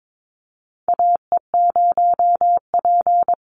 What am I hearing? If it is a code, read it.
Morse code, heard as a keyed tone, about 22 wpm, 700 Hz. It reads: AE0P
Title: Call Signs Repeated 3x - 22wpm